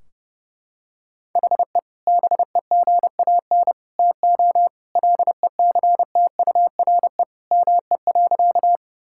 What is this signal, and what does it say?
Morse code, heard as a keyed tone, about 32 wpm, 700 Hz. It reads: HE BEGAN TO LECTURE ME.
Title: Sentences from Top 1600 Words 30wpm